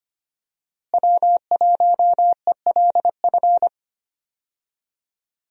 Morse code, heard as a keyed tone, about 25 wpm, 700 Hz. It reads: W1ELF
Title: USA Call Signs - 5 Characters - Repeated 2x - 25wpm